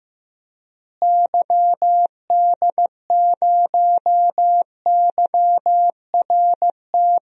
Morse code, heard as a keyed tone, about 15 wpm, 700 Hz. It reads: YD0YRT
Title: DXCC-100 Call Signs - 6 Characters - Repeated 2x - 15wpm